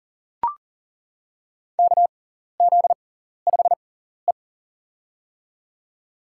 Morse code, heard as a keyed tone, about 40 wpm, spaced as for 14 wpm, 700 Hz. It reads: KZ5E
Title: USA Call Signs - 4 Characters - 6x Character Spacing - 40wpm